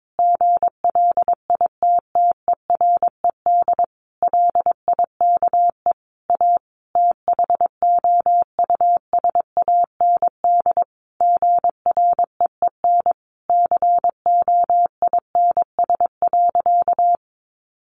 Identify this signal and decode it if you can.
Morse code, heard as a keyed tone, about 22 wpm, 700 Hz. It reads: GLITTERED LIKE A THOUSAND GREEN COINS.
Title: A Sound Of Thunder 22wpm